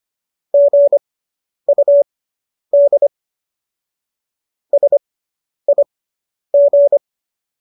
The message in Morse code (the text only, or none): GUD SIG